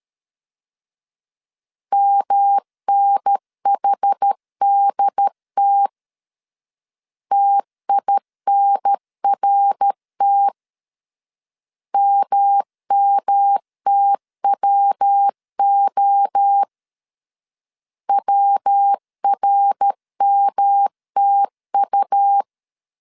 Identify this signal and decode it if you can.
Morse code, heard as a keyed tone, about 13 wpm, 800 Hz. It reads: MNHDT TINRT MMTWO WRMTU
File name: CW / DET / 1 3 1500C S C 12wpm 2750